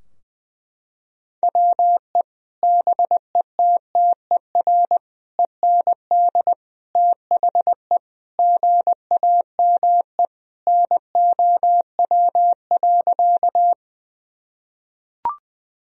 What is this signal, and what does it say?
Morse code, heard as a keyed tone, about 21 wpm, 700 Hz. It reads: WE BETTER END THE GAME NOW.
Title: Sentences from Top 1000 Words - Review Part 3 - 20wpm